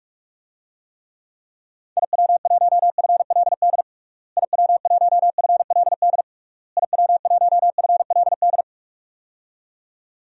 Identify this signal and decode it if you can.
Morse code, heard as a keyed tone, about 46 wpm, 700 Hz. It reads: IW1FLD IW1FLD IW1FLD
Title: DXCC-100 Call Signs - 6 Characters - Repeated 3x - 45wpm